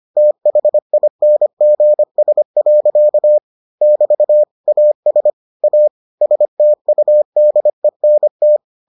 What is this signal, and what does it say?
Morse code, heard as a keyed tone, about 25 wpm, 600 Hz. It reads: THINGS. <BT> AS A STUDENT